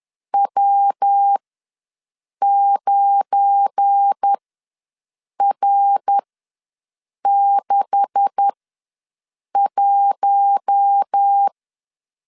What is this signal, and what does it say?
Morse code, heard as a keyed tone, about 10 wpm, spaced as for 6 wpm, 800 Hz. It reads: W9R61
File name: LEZIONI / 9 286 ws 10wpm 4032